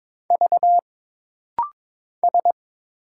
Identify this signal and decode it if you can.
Morse code, heard as a keyed tone, about 22 wpm, 700 Hz. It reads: V S
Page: This rapid-fire timing is geared to push your ICR (Instant Character Recognition) skills.